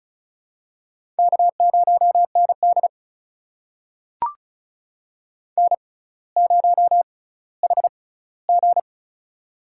Morse code, heard as a keyed tone, about 36 wpm, spaced as for 18 wpm, 700 Hz. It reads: K0ND N0HG
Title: USA Call Signs - 4 Characters - 6x Character Spacing - 35wpm